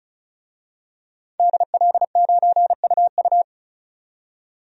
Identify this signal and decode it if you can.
Morse code, heard as a keyed tone, about 36 wpm, 700 Hz. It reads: DL9UU
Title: DXCC-100 Call Signs - 5 Characters - 35wpm